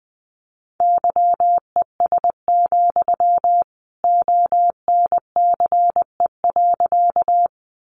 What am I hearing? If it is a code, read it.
Morse code, heard as a keyed tone, about 20 wpm, 700 Hz. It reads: YES, ONCE.
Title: Sentences from Top 500 Words 20wpm